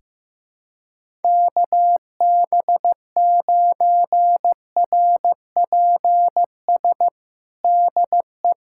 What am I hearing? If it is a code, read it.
Morse code, heard as a keyed tone, about 16 wpm, 700 Hz. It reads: KB9RPS DE
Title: USA Callsign DE Callsign - 6 Characters - 15wpm